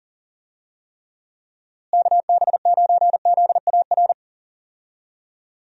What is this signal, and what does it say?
Morse code, heard as a keyed tone, about 40 wpm, 700 Hz. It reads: KB9ZAR
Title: USA Call Signs - 6 Characters - Repeated 3x - 40wpm